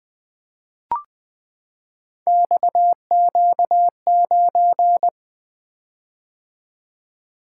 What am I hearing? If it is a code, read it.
Morse code, heard as a keyed tone, about 21 wpm, 700 Hz. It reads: XQ9
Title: Three Letter-Number Combinations - 20wpm